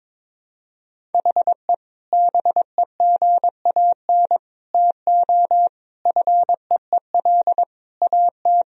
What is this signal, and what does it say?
Morse code, heard as a keyed tone, about 22 wpm, 700 Hz. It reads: HE BEGAN TO FEEL AT
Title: Sentences from Top 1000 Words - Review Part 1 - 22wpm